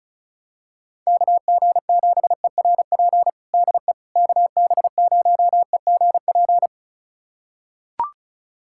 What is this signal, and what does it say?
Morse code, heard as a keyed tone, about 34 wpm, 690 Hz. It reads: KG7ERP DE KB0EGP
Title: USA Callsign DE Callsign 35wpm